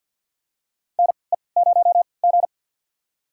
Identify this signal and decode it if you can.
Morse code, heard as a keyed tone, about 51 wpm, spaced as for 28 wpm, 700 Hz. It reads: NE0G
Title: USA Call Signs - 4 Characters - 3x Character Spacing - 50wpm